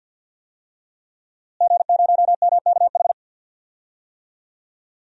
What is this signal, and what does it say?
Morse code, heard as a keyed tone, about 47 wpm, 700 Hz. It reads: G0MKH
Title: DXCC-100 Call Signs - 5 Characters - Repeated 2x - 50wpm